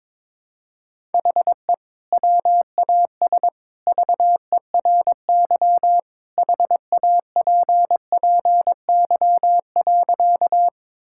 Morse code, heard as a keyed tone, about 22 wpm, 700 Hz. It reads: HE WAS VERY HAPPY.
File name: practice / 22wpm / Sentences with 2-Word Phrases - 22wpm